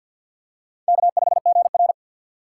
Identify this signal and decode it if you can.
Morse code, heard as a keyed tone, about 52 wpm, 700 Hz. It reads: K5GR
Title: USA Call Signs - 4x Character Spacing - 50wpm